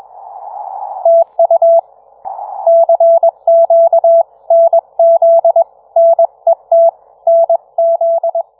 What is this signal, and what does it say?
Morse code, heard as a keyed tone, about 22 wpm, 670 Hz. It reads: TU CQ NZ NET NZ